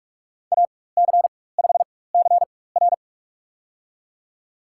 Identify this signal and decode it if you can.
Morse code, heard as a keyed tone, about 42 wpm, spaced as for 20 wpm, 700 Hz. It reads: AC5CR
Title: USA Call Signs - 5 Characters - 4x Character Spacing - 45wpm